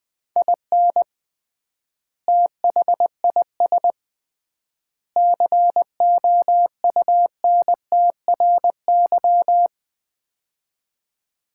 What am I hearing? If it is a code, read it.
Morse code, heard as a keyed tone, about 20 wpm, 700 Hz. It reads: IN THIS COUNTRY
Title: 3x Word-Spacing - Sets of 3 Words - 20wpm